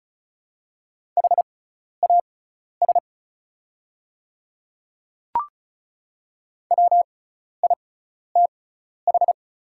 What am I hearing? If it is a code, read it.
Morse code, heard as a keyed tone, about 36 wpm, spaced as for 12 wpm, 700 Hz. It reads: HAS WITH
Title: 6x Character Spacing - Top 100 Words - 35wpm